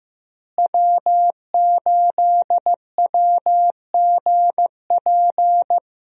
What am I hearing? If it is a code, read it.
Morse code, heard as a keyed tone, about 15 wpm, 700 Hz. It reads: W8WGP
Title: USA Call Signs - 5 Characters - Repeated 3x - 15wpm